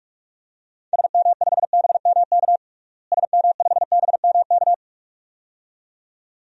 Morse code, heard as a keyed tone, about 42 wpm, 700 Hz. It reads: SM5BMK SM5BMK
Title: DXCC-100 Call Signs - 6 Characters - Repeated 2x - 45wpm